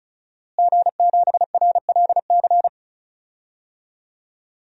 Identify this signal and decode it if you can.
Morse code, heard as a keyed tone, about 36 wpm, 700 Hz. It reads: G7RLC
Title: DXCC-100 Call Signs - 5 Characters - Repeated 2x - 35wpm